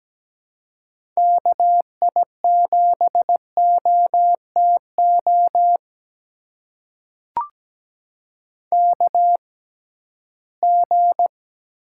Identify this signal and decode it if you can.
Morse code, heard as a keyed tone, about 17 wpm, spaced as for 13 wpm, 700 Hz. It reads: KI7OTO KG
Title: USA Call Signs - 6 Characters - 6x Character Spacing - 17wpm